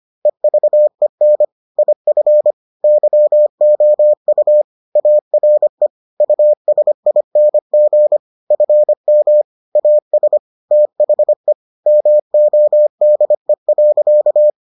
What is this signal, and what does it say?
Morse code, heard as a keyed tone, about 25 wpm, 600 Hz. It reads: EVEN IF YOU ARE USING FM AS THE MODE.